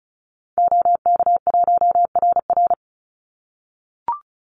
Morse code, heard as a keyed tone, about 34 wpm, 700 Hz. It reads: OK1RR
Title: Call Signs - 35wpm